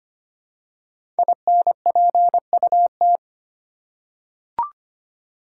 Morse code, heard as a keyed tone, about 25 wpm, 700 Hz. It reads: INPUT
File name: 5 Letter Words - First 3 Letters Given - 25wpm